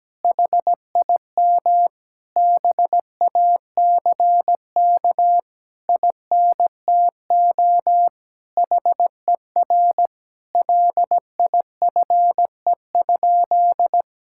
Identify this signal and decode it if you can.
Morse code, heard as a keyed tone, about 17 wpm, 700 Hz. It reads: HIM BACK INTO HER LIFE?
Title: Sentences from Top 1000 Words - Review Part 6 - 17wpm